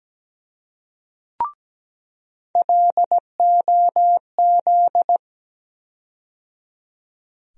Morse code, heard as a keyed tone, about 17 wpm, 700 Hz. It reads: LOZ
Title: USA Call Sign Postfixes - 3 Letters 17wpm